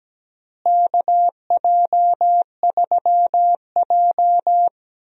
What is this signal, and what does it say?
Morse code, heard as a keyed tone, about 17 wpm, 700 Hz. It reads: KJ3J
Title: USA Call Signs - 4 Characters - Repeated 3x - 17wpm